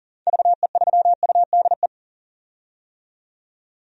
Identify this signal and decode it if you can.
Morse code, heard as a keyed tone, about 40 wpm, 700 Hz. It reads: VE3UDE
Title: DXCC-100 Call Signs - 6 Characters - 40wpm